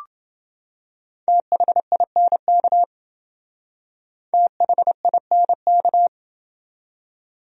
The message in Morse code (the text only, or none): THINK THINK